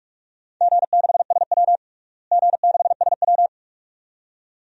Morse code, heard as a keyed tone, about 42 wpm, 700 Hz. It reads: G6SW G6SW
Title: DXCC-100 Call Signs - 4 Characters - Repeated 2x - 45wpm